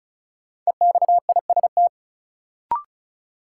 Morse code, heard as a keyed tone, about 34 wpm, 700 Hz. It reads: EXIST
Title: Sentences from Top 1000 Words - Review Part 5 - New Word Drill - 35wpm